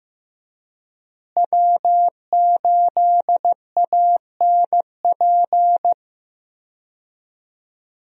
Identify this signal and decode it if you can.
Morse code, heard as a keyed tone, about 16 wpm, 700 Hz. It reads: W8ANP
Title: USA Call Signs - 5 Characters - Repeated 2x - 15wpm